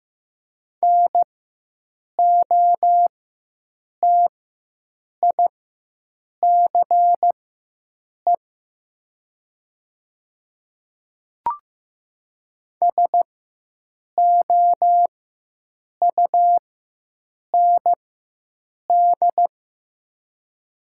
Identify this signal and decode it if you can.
Morse code, heard as a keyed tone, about 15 wpm, spaced as for 7 wpm, 700 Hz. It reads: NOTICE SOUND
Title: 4x Character Spacing - Top 500 Words - 15wpm